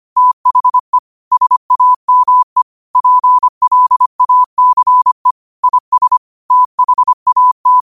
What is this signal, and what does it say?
Morse code, heard as a keyed tone, about 25 wpm, 1000 Hz. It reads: THE SAME PLACE IS THAT